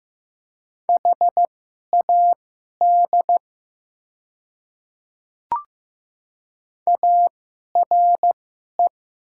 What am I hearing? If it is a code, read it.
Morse code, heard as a keyed tone, about 15 wpm, spaced as for 11 wpm, 700 Hz. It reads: HAD ARE
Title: 2x Character Spacing - Top 100 Words - 15wpm